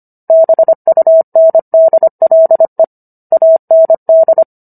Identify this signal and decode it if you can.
Morse code, heard as a keyed tone, about 25 wpm, 650 Hz. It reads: BUNDLE AND